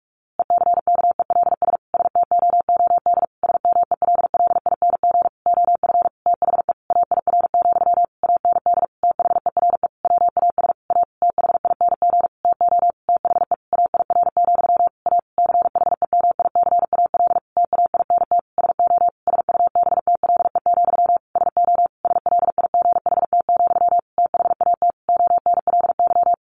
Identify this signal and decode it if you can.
Morse code, heard as a keyed tone, about 44 wpm, 700 Hz. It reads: ECKELS STOOD SMELLING OF THE AIR, AND THERE WAS A THING TO THE AIR, A CHEMICAL TAINT SO SUBTLE, SO SLIGHT, THAT ONLY
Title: A Sound Of Thunder 45wpm